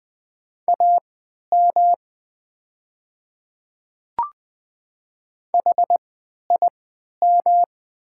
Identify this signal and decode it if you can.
Morse code, heard as a keyed tone, about 20 wpm, spaced as for 11 wpm, 690 Hz. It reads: AM HIM
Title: 3x Character Spacing - Top 500 Words - 20wpm